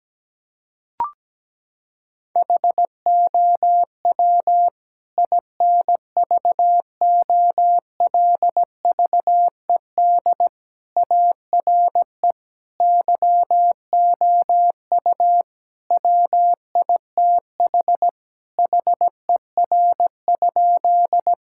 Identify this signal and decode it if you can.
Morse code, heard as a keyed tone, about 17 wpm, 700 Hz. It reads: HOW INVOLVED ARE YOU WITH HER?
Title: Sentences from Top 1000 Words - Review Part 5 - 17wpm